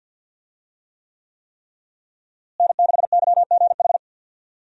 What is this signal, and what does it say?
Morse code, heard as a keyed tone, about 46 wpm, 700 Hz. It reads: N6YGH
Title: USA Call Signs - 5 Characters - Repeated 3x - 50wpm